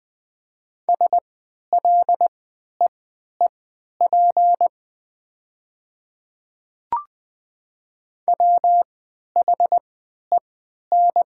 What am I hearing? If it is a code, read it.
Morse code, heard as a keyed tone, about 20 wpm, spaced as for 11 wpm, 700 Hz. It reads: SLEEP WHEN
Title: 3x Character Spacing - Top 500 Words - 20wpm